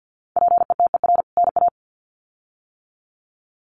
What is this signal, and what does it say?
Morse code, heard as a keyed tone, about 50 wpm, 700 Hz. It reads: PETER NA
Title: CWT Exchange 50wpm